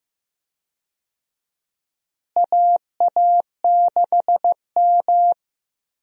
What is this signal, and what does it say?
Morse code, heard as a keyed tone, about 15 wpm, 700 Hz. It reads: AA6M
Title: USA Call Signs - 4 Characters - Repeated 3x - 15wpm